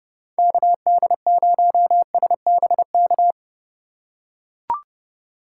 Morse code, heard as a keyed tone, about 30 wpm, 700 Hz. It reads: KD0SBK
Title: USA Call Signs - 6 Characters - 4x Character Spacing - 30wpm